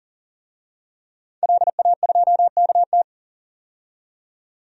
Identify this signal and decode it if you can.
Morse code, heard as a keyed tone, about 40 wpm, 700 Hz. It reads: LA2KT
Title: DXCC-100 Call Signs - 5 Characters - 40wpm